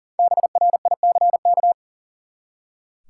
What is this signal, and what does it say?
Morse code, heard as a keyed tone, about 40 wpm, 700 Hz. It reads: BRICK
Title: Five Letter Words - 40wpm